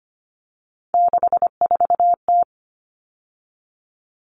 Morse code, heard as a keyed tone, about 25 wpm, 700 Hz. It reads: 64T
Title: Common Cut Numbers - 3-Digit 25wpm